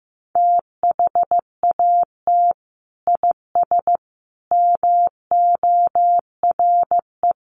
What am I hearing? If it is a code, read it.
Morse code, heard as a keyed tone, about 15 wpm, 700 Hz. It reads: THAT IS MORE
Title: Sentences from Top 1300 Words 15wpm